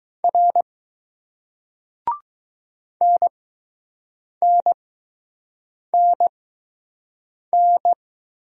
Morse code, heard as a keyed tone, about 20 wpm, spaced as for 6 wpm, 700 Hz. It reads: R NNNN